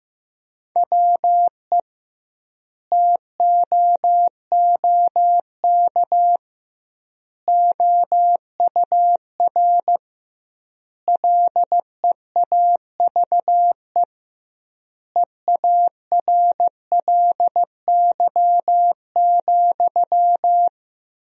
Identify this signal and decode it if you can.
Morse code, heard as a keyed tone, about 15 wpm, 700 Hz. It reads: WE TOOK OUR LEAVE EARLY,
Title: 2x Word-Spacing - Sentences from Top 400 Words - 15wpm